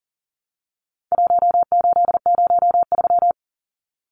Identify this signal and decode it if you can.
Morse code, heard as a keyed tone, about 40 wpm, 700 Hz. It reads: 1803
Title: Numbers - 4-Digit 40wpm